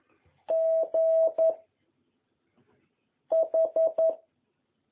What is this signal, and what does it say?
Morse code, heard as a keyed tone, about 11 wpm, 660 Hz. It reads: G H
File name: jh8cbh / 2021cw